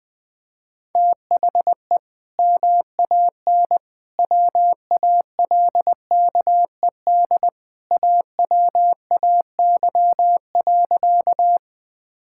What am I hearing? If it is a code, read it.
Morse code, heard as a keyed tone, about 20 wpm, 700 Hz. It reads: THE MAN WALKED AWAY.
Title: Sentences with 2-Word Phrases - 20wpm